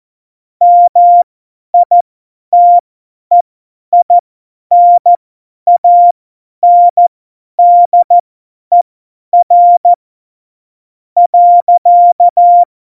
Morse code, heard as a keyed tone, about 14 wpm, spaced as for 10 wpm, 700 Hz. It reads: MITEINANDER .
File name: Morsekurs / Witze / Witz 10 April 2018, 50 F70